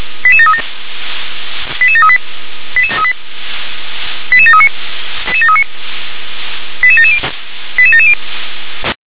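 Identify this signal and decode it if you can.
Five-tone selective call (ZVEI), heard as quick runs of five tones, tones of about 70 ms. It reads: 88737 88738 88739 88770 88770
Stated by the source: ZVEI five-tone selective call received on an SDR.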